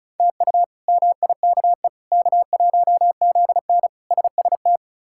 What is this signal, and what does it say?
Morse code, heard as a keyed tone, about 34 wpm, 700 Hz. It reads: TU MIKE K1ZN SST
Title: K1USN SST QSO Practice - 35wpm